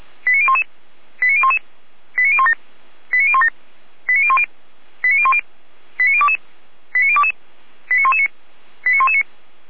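Five-tone selective call (ZVEI), heard as quick runs of five tones, tones of about 80 ms. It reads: 89011 89011 89017 89017 89010 89010 89022 89022 89119 89119
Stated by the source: ZVEI five-tone call 89001 received on an SDR, from the audio server of feuerwehr-orlamuende.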